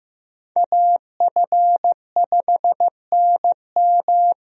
Morse code, heard as a keyed tone, about 15 wpm, 700 Hz. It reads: AF5NM
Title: USA Call Signs - 5 Characters - Repeated 2x - 15wpm